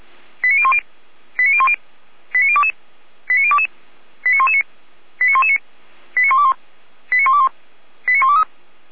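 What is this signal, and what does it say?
Five-tone selective call (ZVEI), heard as quick runs of five tones, tones of about 70 ms. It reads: 89010 89010 89022 89022 89119 89119 89121 89121 89123
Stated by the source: ZVEI five-tone selective call received by SDR.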